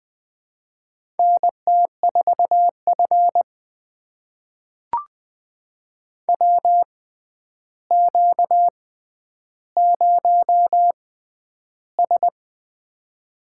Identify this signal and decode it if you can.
Morse code, heard as a keyed tone, about 20 wpm, spaced as for 10 wpm, 700 Hz. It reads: NT4F WQ0S
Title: USA Call Signs - 4 Characters - 6x Character Spacing - 20wpm